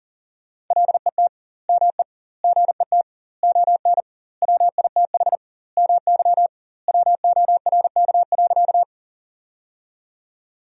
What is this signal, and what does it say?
Morse code, heard as a keyed tone, about 37 wpm, 700 Hz. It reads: LET ME GET ON WITH MY WORK.
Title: 2x Word-Spacing - Sentences from Top 300 Words - 40wpm